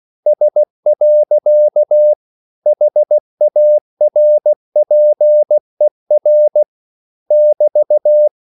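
Morse code, heard as a keyed tone, about 16 wpm, 600 Hz. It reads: S. HARPER <BT>
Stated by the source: English quotations sent in CW as practice at 16 words per minute.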